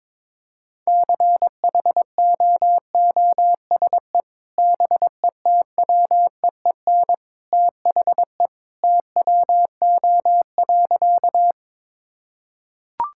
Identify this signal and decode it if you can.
Morse code, heard as a keyed tone, about 21 wpm, 700 Hz. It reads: CHOOSE BETWEEN THE TWO.
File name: practice / 22wpm / Sentences with 2-Word Phrases - 22wpm